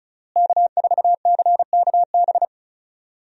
Morse code, heard as a keyed tone, about 36 wpm, 700 Hz. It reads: K4CKB
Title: USA Call Signs - 5 Characters - Repeated 3x - 35wpm